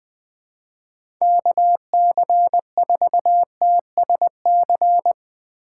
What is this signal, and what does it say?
Morse code, heard as a keyed tone, about 20 wpm, 700 Hz. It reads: KC4TSC
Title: USA Call Signs - 6 Characters - 6x Character Spacing - 20wpm